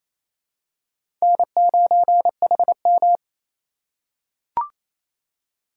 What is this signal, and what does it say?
Morse code, heard as a keyed tone, about 28 wpm, 700 Hz. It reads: N9HM
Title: USA Call Signs - 4 Characters - 4x Character Spacing - 28wpm